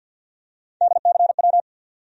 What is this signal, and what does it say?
Morse code, heard as a keyed tone, about 53 wpm, 700 Hz. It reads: DCW
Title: USA Call Sign Postfixes - 3 Letters 50wpm